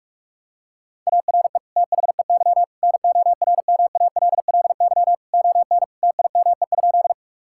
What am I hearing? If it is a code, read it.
Morse code, heard as a keyed tone, about 46 wpm, 700 Hz. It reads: ARE THEY NORMALLY ON TIME?
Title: Sentences from Top 1000 Words - Review Part 6 - 45wpm